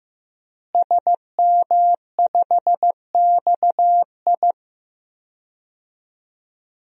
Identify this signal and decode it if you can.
Morse code, heard as a keyed tone, about 16 wpm, 700 Hz. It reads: SM5XI
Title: DXCC-100 Call Signs - 5 Characters - Repeated 2x - 15wpm